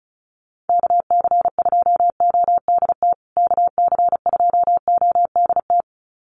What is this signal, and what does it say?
Morse code, heard as a keyed tone, about 34 wpm, 700 Hz. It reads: KC2ODT KC2ODT
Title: USA Call Signs Repeated 2x 35wpm